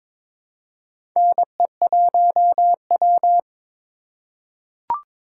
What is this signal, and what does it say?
Morse code, heard as a keyed tone, about 21 wpm, 700 Hz. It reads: NE1W
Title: USA Call Signs - 4 Characters - 4x Character Spacing - 22wpm